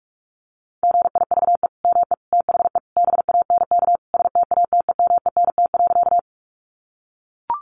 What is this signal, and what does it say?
Morse code, heard as a keyed tone, about 44 wpm, 700 Hz. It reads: GIVE ME THE BANK STATEMENT.
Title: Sentences from Top 1200 Words 45wpm